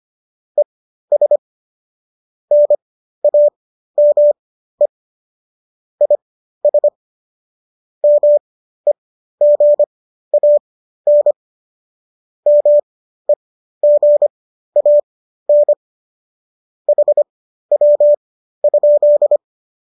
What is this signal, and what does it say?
Morse code, heard as a keyed tone, about 25 wpm, spaced as for 13 wpm, 590 Hz. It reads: ES NAME IS MEGAN MEGAN HW?